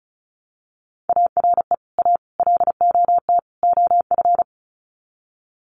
Morse code, heard as a keyed tone, about 34 wpm, 700 Hz. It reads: ARE A LOT OF
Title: Sets of 4 Words 35wpm